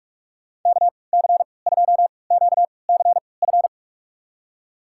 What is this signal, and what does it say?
Morse code, heard as a keyed tone, about 46 wpm, spaced as for 25 wpm, 700 Hz. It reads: KC2QCF
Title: USA Call Signs - 3x Character Spacing - 45wpm